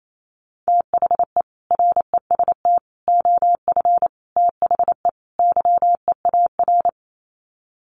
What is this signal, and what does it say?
Morse code, heard as a keyed tone, about 28 wpm, 700 Hz. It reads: THE REST OF THE YEAR
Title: Sets of 5 Words Encore 28wpm